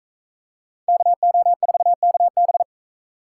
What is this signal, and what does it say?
Morse code, heard as a keyed tone, about 38 wpm, 700 Hz. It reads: KO4KB